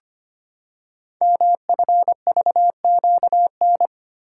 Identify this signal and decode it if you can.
Morse code, heard as a keyed tone, about 25 wpm, 700 Hz. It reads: MFVQN
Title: Code Groups - 5 Letters - 25wpm